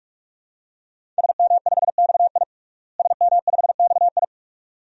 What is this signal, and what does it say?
Morse code, heard as a keyed tone, about 46 wpm, 700 Hz. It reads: SM5XI SM5XI
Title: DXCC-100 Call Signs - 5 Characters - Repeated 2x - 45wpm